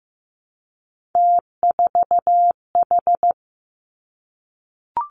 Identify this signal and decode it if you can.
Morse code, heard as a keyed tone, about 15 wpm, 700 Hz. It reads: T4H
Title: Three Challenging Letter-Number Combinations 15wpm